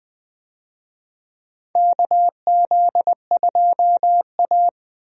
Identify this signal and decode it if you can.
Morse code, heard as a keyed tone, about 20 wpm, 700 Hz. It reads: KZ2A
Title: USA Call Signs - 4 Characters - Repeated 3x - 20wpm